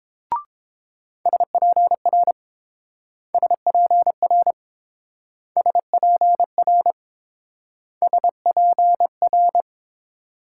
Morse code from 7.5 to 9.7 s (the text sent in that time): SPR